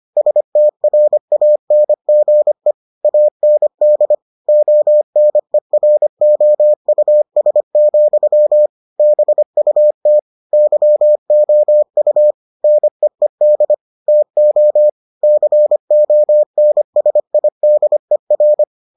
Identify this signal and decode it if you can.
Morse code, heard as a keyed tone, about 25 wpm, 600 Hz. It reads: STRANGE AND ONEROUS, BUT YOU NEED TO CONSIDER